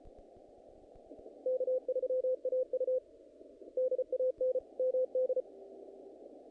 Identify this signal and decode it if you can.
Morse code, heard as a keyed tone, about 33 wpm, 510 Hz. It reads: K3AU DAN MD
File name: contests / naqp-cw-k5tr-op-k5pi- -2000